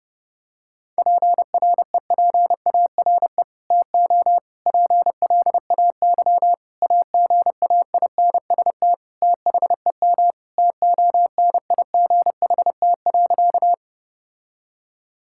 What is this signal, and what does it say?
Morse code, heard as a keyed tone, about 30 wpm, 700 Hz. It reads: PREPARE TO PLAY AGAINST THEM TONIGHT.
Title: Sentences from Top 1000 Words - Review Part 6 - 30wpm